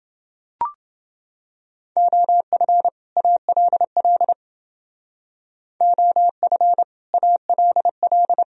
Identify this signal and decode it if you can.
Morse code, heard as a keyed tone, about 29 wpm, 700 Hz. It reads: OF ALL OF ALL